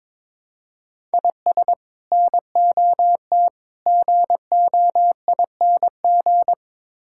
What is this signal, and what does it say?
Morse code, heard as a keyed tone, about 22 wpm, 700 Hz. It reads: IS NOT GOING
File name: practice / 22wpm / Build A Phrase - 5-Word Phrases - 22wpm